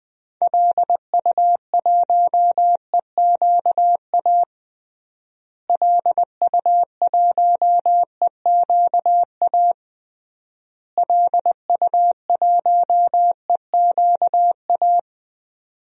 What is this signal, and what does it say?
Morse code, heard as a keyed tone, about 20 wpm, 700 Hz. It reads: LU1EQA LU1EQA LU1EQA
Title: DXCC-100 Call Signs - 6 Characters - Repeated 3x - 20wpm